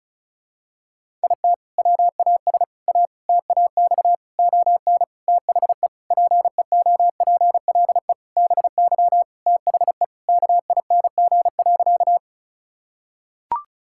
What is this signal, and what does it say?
Morse code, heard as a keyed tone, about 36 wpm, 700 Hz. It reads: IT WAS A TAX ON THE PEOPLE BY THE KING.
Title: Sentences from Top 1000 Words - Review Part 5 - 35wpm